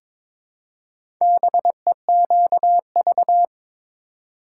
Morse code, heard as a keyed tone, about 22 wpm, 700 Hz. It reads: BEQV
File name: practice / 22wpm / Four Letter-Number Combinations - 22wpm